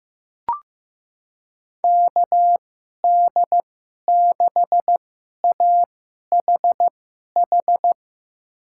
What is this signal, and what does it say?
Morse code, heard as a keyed tone, about 15 wpm, spaced as for 11 wpm, 700 Hz. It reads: KD6AHH
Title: USA Call Signs - 2x Character Spacing - 15wpm